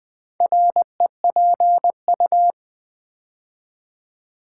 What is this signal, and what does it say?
Morse code, heard as a keyed tone, about 20 wpm, 700 Hz. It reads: REPU
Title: Build A Word - 4 Letters At A Time - 8-Letter Words - 20wpm